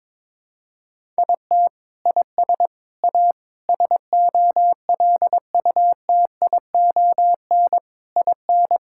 Morse code, heard as a keyed tone, about 22 wpm, 700 Hz. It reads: IT IS A SOLUTION IN
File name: Sentences with 7-Word Phrases - 22wpm